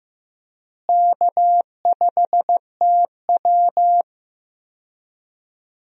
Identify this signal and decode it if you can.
Morse code, heard as a keyed tone, about 15 wpm, 700 Hz. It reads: K5TW
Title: USA Call Signs - 4 Characters - Repeated 3x - 15wpm